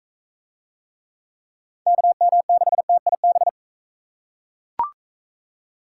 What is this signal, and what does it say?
Morse code, heard as a keyed tone, about 43 wpm, 700 Hz. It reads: KM6TIB